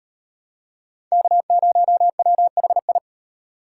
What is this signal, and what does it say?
Morse code, heard as a keyed tone, about 38 wpm, 700 Hz. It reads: K0WHI